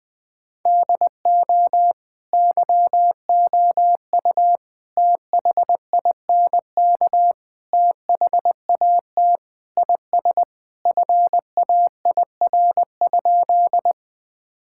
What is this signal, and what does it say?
Morse code, heard as a keyed tone, about 20 wpm, 700 Hz. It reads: DO YOU THINK THAT IS FAIR?
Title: Sentences with 4-Word Phrases - Spoken Beginning - 20wpm